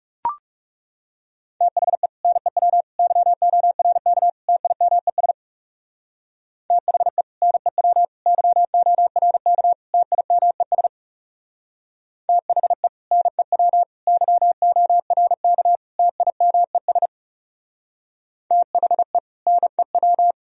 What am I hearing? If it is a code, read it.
Morse code, heard as a keyed tone, about 35 wpm, 700 Hz. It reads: THE NEW YORK TIMES THE NEW YORK TIMES THE NEW YORK TIMES THE NEW
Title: Speed-Racing - Sets of 4 Words 30wpm